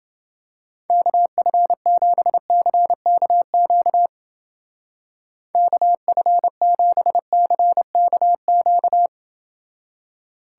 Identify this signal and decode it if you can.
Morse code, heard as a keyed tone, about 28 wpm, 700 Hz. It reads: KF7CKQ KF7CKQ